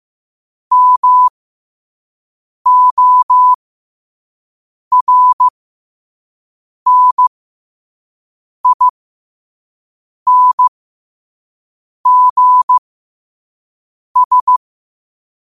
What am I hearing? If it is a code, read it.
Morse code, heard as a keyed tone, about 15 wpm, spaced as for 5 wpm, 1000 Hz. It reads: MORNINGS
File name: qotdfiles / 5wpm